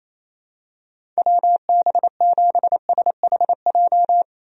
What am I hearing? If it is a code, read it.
Morse code, heard as a keyed tone, about 28 wpm, 700 Hz. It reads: WB7SHJ
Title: USA Call Signs - 6 Characters - 28wpm